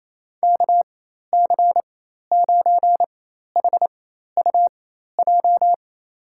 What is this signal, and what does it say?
Morse code, heard as a keyed tone, about 28 wpm, spaced as for 13 wpm, 700 Hz. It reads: KC9HUJ
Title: USA Call Signs - 6 Characters - 4x Character Spacing - 28wpm